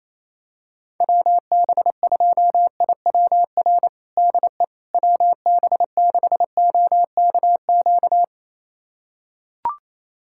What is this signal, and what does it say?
Morse code, heard as a keyed tone, about 28 wpm, 700 Hz. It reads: WB2IWR DE WB6OKQ
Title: USA Callsign DE Callsign 28wpm